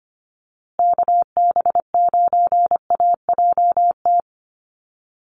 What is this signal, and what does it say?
Morse code, heard as a keyed tone, about 25 wpm, 700 Hz. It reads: KB9AJT
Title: USA Call Signs Repeated 3x 25wpm